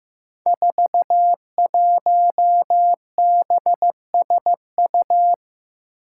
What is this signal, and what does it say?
Morse code, heard as a keyed tone, about 15 wpm, 700 Hz. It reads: 41BSU
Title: Five Challenging Letter-Number Combinations 15wpm